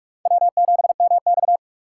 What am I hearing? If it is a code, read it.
Morse code, heard as a keyed tone, about 46 wpm, 700 Hz. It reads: W7MX
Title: USA Call Signs - 4 Characters - 2x Character Spacing - 45wpm